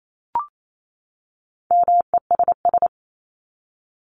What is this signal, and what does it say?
Morse code, heard as a keyed tone, about 28 wpm, 700 Hz. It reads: MESS
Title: Top 1751-2000 Words 28wpm